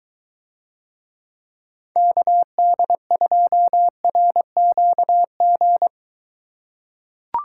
Morse code, heard as a keyed tone, about 23 wpm, 700 Hz. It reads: KD2RQG